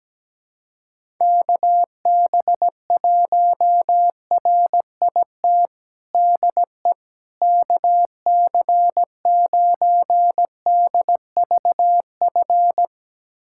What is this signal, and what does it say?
Morse code, heard as a keyed tone, about 17 wpm, 700 Hz. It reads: KB1RIT DE KC9DVF
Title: USA Callsign DE Callsign - 6 Characters - 17wpm